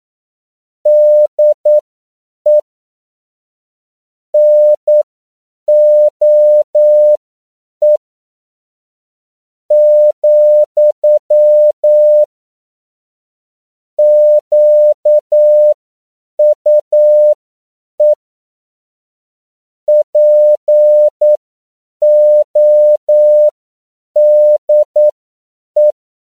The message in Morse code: DE NOE , QUE PODE

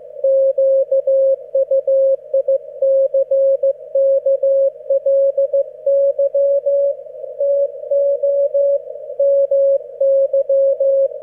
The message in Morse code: QUICKLY TO MY

